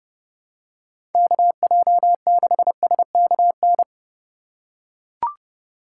Morse code, heard as a keyed tone, about 31 wpm, 700 Hz. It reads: KJ6SKN